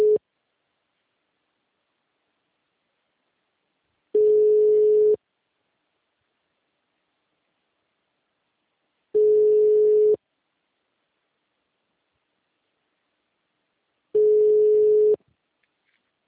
Ringback tone (Europe, 425 Hz).
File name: out- -2011- - - .30